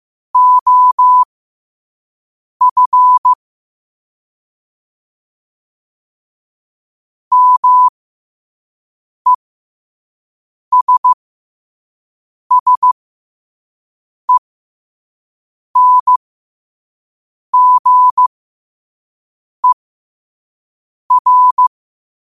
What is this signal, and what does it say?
Morse code, heard as a keyed tone, about 15 wpm, spaced as for 5 wpm, 1000 Hz. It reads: OF MESSENGER
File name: qotdfiles / 5wpm